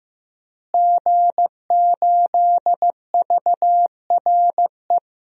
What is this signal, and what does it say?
Morse code, heard as a keyed tone, about 16 wpm, 700 Hz. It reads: G8VRE
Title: DXCC-100 Call Signs - 5 Characters - Repeated 3x - 15wpm